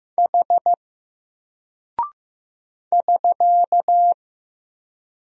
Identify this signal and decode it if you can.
Morse code, heard as a keyed tone, about 15 wpm, 700 Hz. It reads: H <SK>
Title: Single Character (Letter:Number:Punctuation:Prosign) 15wpm